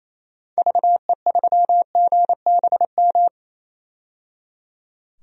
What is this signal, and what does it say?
Morse code, heard as a keyed tone, about 27 wpm, 700 Hz. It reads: VE3GBM
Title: DXCC-100 Call Signs - 6 Characters - 28wpm